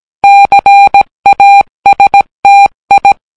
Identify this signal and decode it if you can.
Morse code, heard as a keyed tone, about 17 wpm, 800 Hz. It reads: CASTI